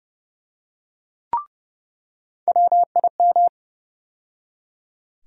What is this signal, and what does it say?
Morse code, heard as a keyed tone, about 29 wpm, 700 Hz. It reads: WIM
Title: USA Call Sign Postfixes - 3 Letters 30wpm